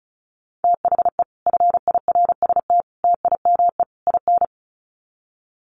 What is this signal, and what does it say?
Morse code, heard as a keyed tone, about 34 wpm, 700 Hz. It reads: THE FIRST TIME IN
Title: Sets of 4 Words 35wpm